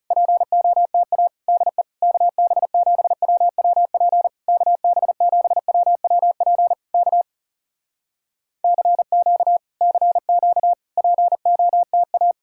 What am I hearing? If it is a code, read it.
Morse code, heard as a keyed tone, about 36 wpm, 700 Hz. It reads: POTA DE KB7WWP KB7WWP K CQ CQ POTA
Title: Speed-Racing Common QSO Phrases - 30wpm